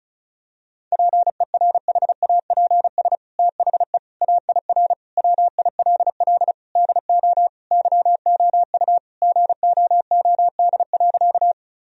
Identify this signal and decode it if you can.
Morse code, heard as a keyed tone, about 35 wpm, 700 Hz. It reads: PERHAPS THE AIR WILL DO YOU GOOD.
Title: Sentences from Top 1000 Words - Review Part 1 - 35wpm